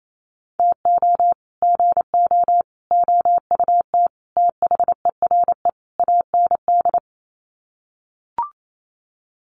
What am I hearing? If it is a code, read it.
Morse code, heard as a keyed tone, about 28 wpm, 700 Hz. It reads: TO GO OUT THERE AND
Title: Sets of 5 Words Encore2 28wpm